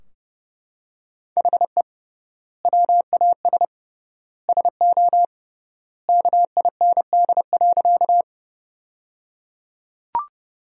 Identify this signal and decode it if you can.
Morse code, heard as a keyed tone, about 30 wpm, 700 Hz. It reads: HE WAS SO KIND.
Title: 3x Word-Spacing - Sentences from Top 200 Words - 30wpm